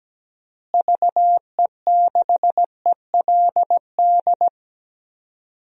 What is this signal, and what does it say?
Morse code, heard as a keyed tone, about 17 wpm, 700 Hz. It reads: VE6ELD
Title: DXCC-100 Call Signs - 6 Characters - Repeated 3x - 17wpm